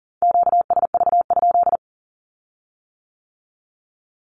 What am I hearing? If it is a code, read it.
Morse code, heard as a keyed tone, about 38 wpm, 700 Hz. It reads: QSV?
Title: Q-Codes with Spoken Meaning - 40wpm